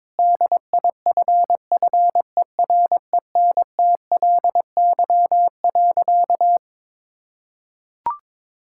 Morse code, heard as a keyed tone, about 22 wpm, 700 Hz. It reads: DIFFERENTLY.
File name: practice / 22wpm / Sentences with 3-Word Phrases - 22wpm